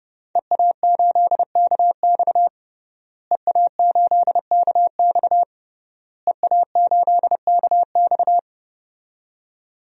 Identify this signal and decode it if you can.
Morse code, heard as a keyed tone, about 30 wpm, 700 Hz. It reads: EA8KX EA8KX EA8KX
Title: DXCC-100 Call Signs - 5 Characters - Repeated 3x - 30wpm